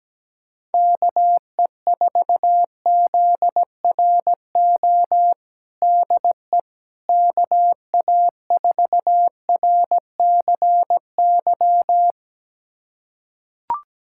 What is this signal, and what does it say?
Morse code, heard as a keyed tone, about 17 wpm, 700 Hz. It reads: KE4ZRO DE KA4RCY
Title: USA Callsign DE Callsign 17wpm